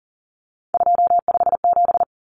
Morse code, heard as a keyed tone, about 40 wpm, 700 Hz. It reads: 257
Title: Common Cut Numbers - 3-Digit 40wpm